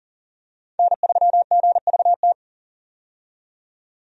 Morse code, heard as a keyed tone, about 40 wpm, 700 Hz. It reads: N3GVT
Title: USA Call Signs - 5 Characters - 40wpm